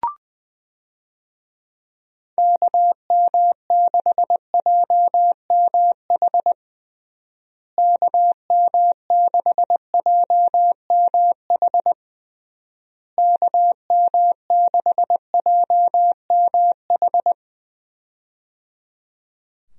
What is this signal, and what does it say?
Morse code, heard as a keyed tone, about 20 wpm, 700 Hz. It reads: KM6JMH KM6JMH KM6JMH
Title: USA Call Signs - 6 Characters - Repeated 3x - 20wpm